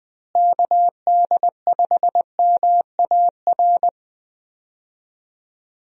Morse code, heard as a keyed tone, about 20 wpm, 700 Hz. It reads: KD5MAR
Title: USA Call Signs - 6 Characters - Repeated 3x - 20wpm